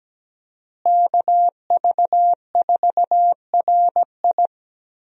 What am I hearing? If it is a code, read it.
Morse code, heard as a keyed tone, about 17 wpm, 700 Hz. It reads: KV4RI
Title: USA Call Signs - 5 Characters - Repeated 3x - 17wpm